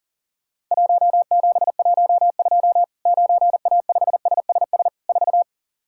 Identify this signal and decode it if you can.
Morse code, heard as a keyed tone, about 40 wpm, 700 Hz. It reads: 1712 9A5SSS 4
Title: Worked All Europe DX Contest - QTC Exchange - 40wpm